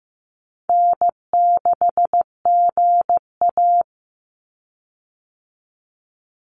Morse code, heard as a keyed tone, about 15 wpm, 700 Hz. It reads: N6GA
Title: Easy Call Signs Repeated 3x - 15wpm